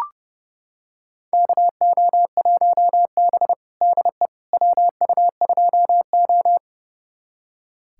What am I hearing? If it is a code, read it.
Morse code, heard as a keyed tone, about 30 wpm, 700 Hz. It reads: KO1B DE WU2O
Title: USA Callsign DE Callsign - 4 Characters - 30wpm